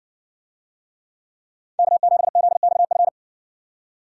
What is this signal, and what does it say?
Morse code, heard as a keyed tone, about 60 wpm, 700 Hz. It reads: K7ZXF